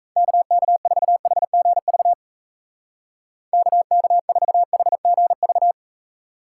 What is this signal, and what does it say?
Morse code, heard as a keyed tone, about 34 wpm, 700 Hz. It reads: KK4HGV KK4HGV
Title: Speed Racing - USA 2x3 Call Signs 28wpm